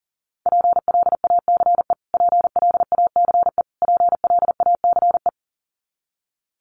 Morse code, heard as a keyed tone, about 40 wpm, 700 Hz. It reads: PLACE PLACE PLACE
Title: Top 500 Words Repeated 40wpm